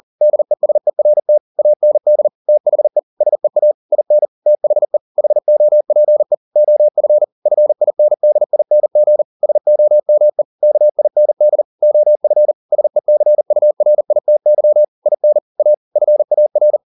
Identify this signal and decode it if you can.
Morse code, heard as a keyed tone, about 40 wpm, 600 Hz. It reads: DESERT AND THE SEA IN THE HOPE OF FINDING SOME KIND OF SECURITY IN A FAR